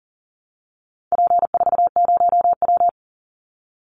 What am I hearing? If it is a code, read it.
Morse code, heard as a keyed tone, about 40 wpm, 700 Hz. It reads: P40W
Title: Call Signs Repeated 2x - 40wpm